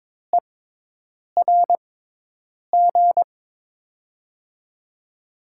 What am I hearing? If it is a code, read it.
Morse code, heard as a keyed tone, about 22 wpm, spaced as for 8 wpm, 700 Hz. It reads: ERG
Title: USA Call Signs - 6 Characters - 6x Character Spacing - 22wpm